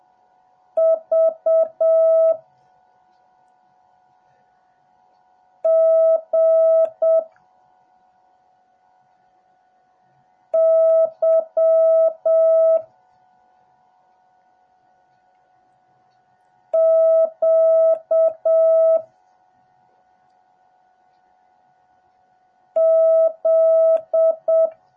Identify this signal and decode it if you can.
Morse code, heard as a keyed tone, about 7 wpm, 650 Hz. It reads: V G Y Q Z
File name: jh8cbh / 2021cw / cw08a